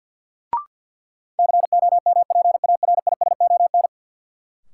Morse code, heard as a keyed tone, about 46 wpm, 700 Hz. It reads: COMPARISON
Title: Sentences from Top 1600 Words - New Word Drill - 50wpm